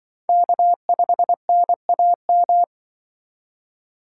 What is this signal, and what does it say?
Morse code, heard as a keyed tone, about 25 wpm, 700 Hz. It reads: K5NAM
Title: Speed Racing - USA 1x3 Call Signs 20wpm